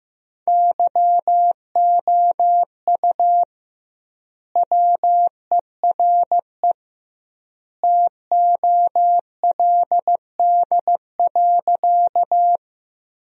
Morse code, heard as a keyed tone, about 15 wpm, 700 Hz. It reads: YOU WERE TOLD.
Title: 2x Word-Spacing - Sentences from Top 400 Words - 15wpm